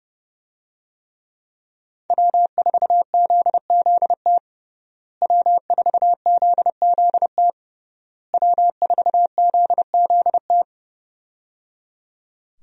Morse code, heard as a keyed tone, about 30 wpm, 700 Hz. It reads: W4ZZT W4ZZT W4ZZT
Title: USA Call Signs - 5 Characters - Repeated 3x - 30wpm